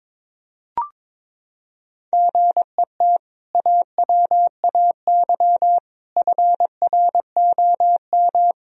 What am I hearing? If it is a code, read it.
Morse code, heard as a keyed tone, about 22 wpm, 700 Hz. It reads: GET AWAY FROM
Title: Sentences from Top 1000 Words - Review Part 2 - 22wpm